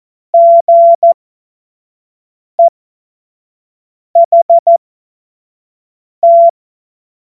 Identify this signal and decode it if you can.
Morse code, heard as a keyed tone, about 14 wpm, spaced as for 5 wpm, 680 Hz. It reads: GEHT
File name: Morsekurs / Witze / Witz 01. Februar 2022 25 F70